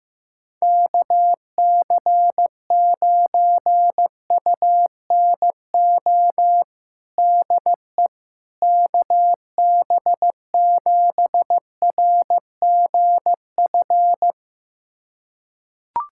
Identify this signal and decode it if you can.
Morse code, heard as a keyed tone, about 15 wpm, 700 Hz. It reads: KC9UNO DE KB7RGF
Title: USA Callsign DE Callsign 15wpm